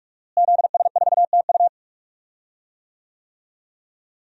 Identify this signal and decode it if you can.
Morse code, heard as a keyed tone, about 41 wpm, 700 Hz. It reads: ZS4TU
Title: DXCC-100 Call Signs - 5 Characters - Repeated 3x - 45wpm